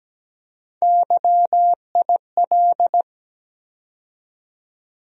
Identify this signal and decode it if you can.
Morse code, heard as a keyed tone, about 17 wpm, 700 Hz. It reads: YIL
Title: USA Call Sign Postfixes - 3 Letters 17wpm